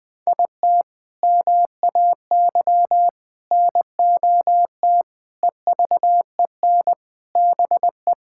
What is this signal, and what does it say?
Morse code, heard as a keyed tone, about 19 wpm, 700 Hz. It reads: IT MAY NOT EVEN BE
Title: Sentences from Top 1000 Words - Review Part 6 - 20wpm